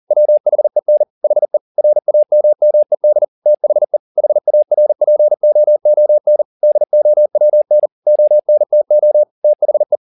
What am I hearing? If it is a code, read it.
Morse code, heard as a keyed tone, about 40 wpm, 600 Hz. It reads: WHEN HE RAMMED THE HARPOON DOWN ONTO THE
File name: the old man and the sea 2 40wpm